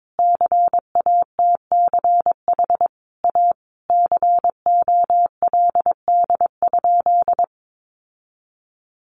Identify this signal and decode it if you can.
Morse code, heard as a keyed tone, about 22 wpm, 700 Hz. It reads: CATCH A COLD?
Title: Sentences from Top 1500 Words 22wpm